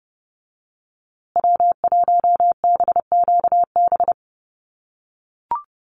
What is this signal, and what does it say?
Morse code, heard as a keyed tone, about 30 wpm, 700 Hz. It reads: W1BQB
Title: USA Call Signs Repeated 2x 30wpm